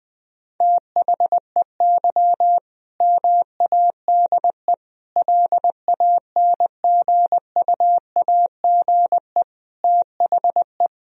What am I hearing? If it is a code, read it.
Morse code, heard as a keyed tone, about 21 wpm, 700 Hz. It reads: THEY MADE LANGUAGE THE
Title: Sentences from Top 1000 Words - Review Part 3 - 20wpm